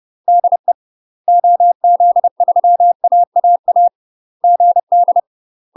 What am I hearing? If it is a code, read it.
Morse code, heard as a keyed tone, about 30 wpm, 700 Hz. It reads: DE OZ3AAA GD
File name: UK MASTER QSOs TRAINING / 30WPM / UK MASTER QSOs 0029